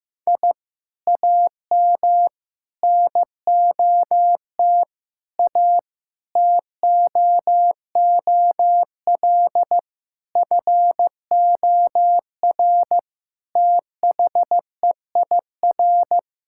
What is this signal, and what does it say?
Morse code, heard as a keyed tone, about 15 wpm, 700 Hz. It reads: I AM NOT A TOOL FOR THEIR
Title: Sentences from Top 1000 Words - Review Part 6 - 15wpm